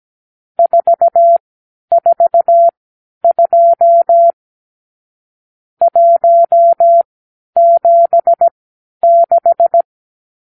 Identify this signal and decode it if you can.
Morse code, heard as a keyed tone, about 17 wpm, spaced as for 10 wpm, 680 Hz. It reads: 442 176
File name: k7qo / Lesson 52 (8)